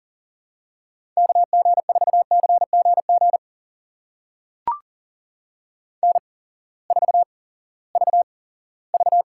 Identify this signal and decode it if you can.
Morse code, heard as a keyed tone, about 40 wpm, spaced as for 20 wpm, 700 Hz. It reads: KG4CGG N4VV
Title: USA Call Signs - 8x Character Spacing - 40wpm